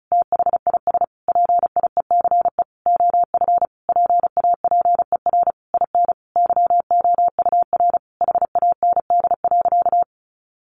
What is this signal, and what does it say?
Morse code, heard as a keyed tone, about 34 wpm, 700 Hz. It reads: THIS PIECE OF PAPER IN YOUR HAND.
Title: Sentences from Top 400 Words 35wpm